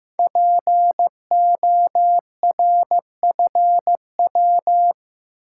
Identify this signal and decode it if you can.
Morse code, heard as a keyed tone, about 15 wpm, 700 Hz. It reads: PORFW
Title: Code Groups - 5 Letters - 15wpm